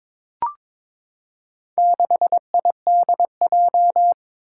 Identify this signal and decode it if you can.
Morse code, heard as a keyed tone, about 22 wpm, 700 Hz. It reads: 6IDJ
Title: Four Challenging Letter-Number Combinations 22wpm